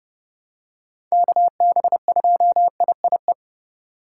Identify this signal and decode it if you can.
Morse code, heard as a keyed tone, about 30 wpm, 700 Hz. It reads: KB2IIE
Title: USA Call Signs - 6 Characters - 4x Character Spacing - 30wpm